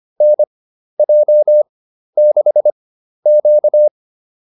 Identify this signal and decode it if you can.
Morse code, heard as a keyed tone, about 25 wpm, spaced as for 12 wpm, 600 Hz. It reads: NJ6Q